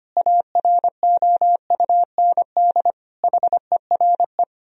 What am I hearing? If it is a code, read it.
Morse code, heard as a keyed tone, about 25 wpm, 700 Hz. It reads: AROUND HERE
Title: Sentences from Top 1000 Words - Review Part 3 - 25wpm